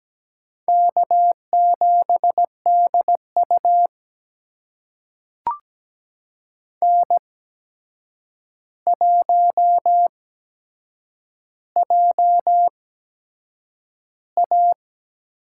Morse code, heard as a keyed tone, about 17 wpm, spaced as for 7 wpm, 700 Hz. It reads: K7DU N1JA
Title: USA Call Signs - 4 Characters - 8x Character Spacing - 17wpm